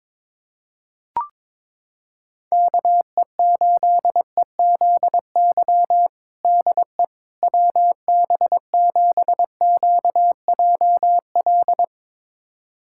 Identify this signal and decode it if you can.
Morse code, heard as a keyed tone, about 22 wpm, 700 Hz. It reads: KE8EZY DE WB7QJL
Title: USA Callsign DE Callsign - 6 Characters - 22wpm